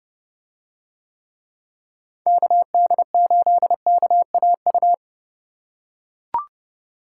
Morse code, heard as a keyed tone, about 28 wpm, 700 Hz. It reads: KD8KAU